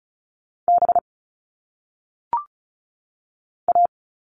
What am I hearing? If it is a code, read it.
Morse code, heard as a keyed tone, about 36 wpm, 700 Hz. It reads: B A
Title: All Cut Numbers - 1-Digit 35wpm